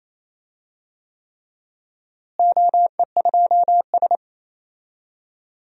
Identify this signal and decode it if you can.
Morse code, heard as a keyed tone, about 28 wpm, 700 Hz. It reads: OE2S
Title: DXCC-100 Call Signs - 4 Characters - Repeated 3x - 28wpm